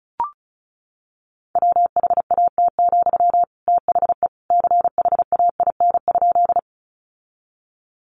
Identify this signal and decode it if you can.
Morse code, heard as a keyed tone, about 34 wpm, 700 Hz. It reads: WHAT, THE CHAIN?
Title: Sentences from Top 1500 Words 35wpm